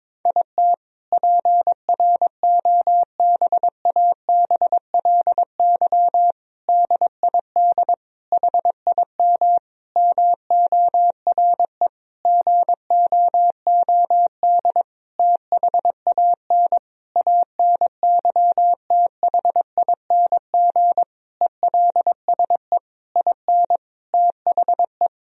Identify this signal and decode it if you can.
Morse code, heard as a keyed tone, about 22 wpm, 700 Hz. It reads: IT PROBABLY DID HIM MORE GOOD THAN ANYTHING ELSE IN THE
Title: Sentences from Top 1000 Words - Review Part 4 - 22wpm